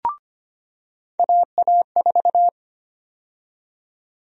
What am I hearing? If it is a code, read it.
Morse code, heard as a keyed tone, about 25 wpm, 700 Hz. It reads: AA4
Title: USA Call Sign Prefixes - 25wpm